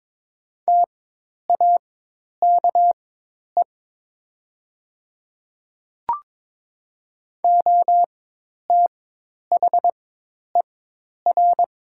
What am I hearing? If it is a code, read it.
Morse code, heard as a keyed tone, about 22 wpm, spaced as for 10 wpm, 700 Hz. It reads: TAKE OTHER
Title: 4x Character Spacing - Top 300 Words - 22wpm